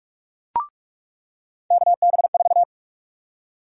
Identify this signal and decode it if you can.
Morse code, heard as a keyed tone, about 46 wpm, 700 Hz. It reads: KB4
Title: USA Call Sign Prefixes - 45wpm